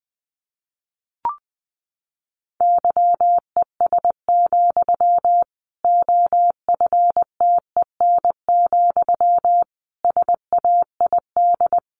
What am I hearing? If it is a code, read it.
Morse code, heard as a keyed tone, about 20 wpm, 700 Hz. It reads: YES, OFTEN, SAID
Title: Sentences from Top 1200 Words 20wpm